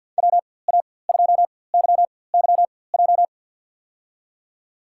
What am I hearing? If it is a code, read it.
Morse code, heard as a keyed tone, about 52 wpm, spaced as for 23 wpm, 700 Hz. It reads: WA2YYJ
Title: USA Call Signs - 6 Characters - 4x Character Spacing - 50wpm